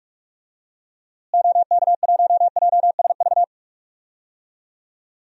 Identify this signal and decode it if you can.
Morse code, heard as a keyed tone, about 47 wpm, 700 Hz. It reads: OK1JSV
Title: DXCC-100 Call Signs - 6 Characters - Repeated 2x - 45wpm